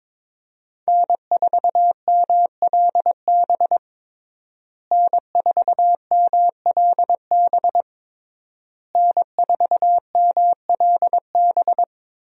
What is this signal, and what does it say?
Morse code, heard as a keyed tone, about 22 wpm, 700 Hz. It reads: N4MLB N4MLB N4MLB
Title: USA Call Signs - 5 Characters - Repeated 3x - 22wpm